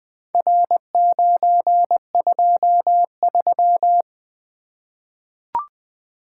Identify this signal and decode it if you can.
Morse code, heard as a keyed tone, about 20 wpm, 700 Hz. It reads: R923
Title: Four Letter-Number Combinations - 20wpm